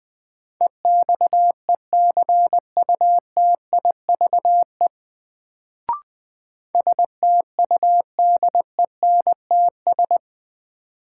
Morse code, heard as a keyed tone, about 20 wpm, 700 Hz. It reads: EXECUTIVE STUDENTS
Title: Sets of 5 Words Encore - New Word Drill - 20wpm